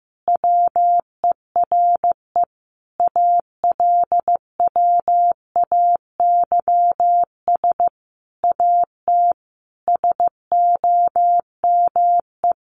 Morse code, heard as a keyed tone, about 15 wpm, 700 Hz. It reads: WERE ALWAYS AT SOME
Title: Sentences from Top 1200 Words 15wpm